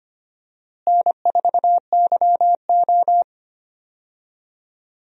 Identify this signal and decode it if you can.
Morse code, heard as a keyed tone, about 25 wpm, 700 Hz. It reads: N4YO
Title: USA Call Signs - 4 Characters - Repeated 3x - 25wpm